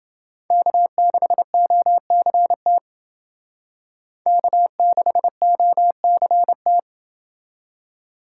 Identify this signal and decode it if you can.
Morse code, heard as a keyed tone, about 30 wpm, 700 Hz. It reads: K6OCT K6OCT